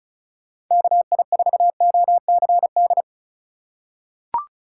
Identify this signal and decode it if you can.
Morse code, heard as a keyed tone, about 36 wpm, 700 Hz. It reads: KI4OCD
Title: USA Call Signs - 6 Characters - 2x Character Spacing - 35wpm